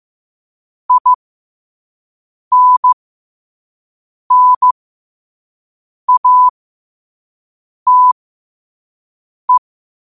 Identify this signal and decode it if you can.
Morse code, heard as a keyed tone, about 15 wpm, spaced as for 5 wpm, 1000 Hz. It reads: INNATE
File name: qotdfiles / 5wpm